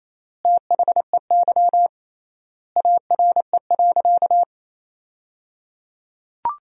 Morse code, heard as a keyed tone, about 28 wpm, 700 Hz. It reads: THEY ARE.
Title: 3x Word-Spacing - Sentences from Top 100 Words - 28wpm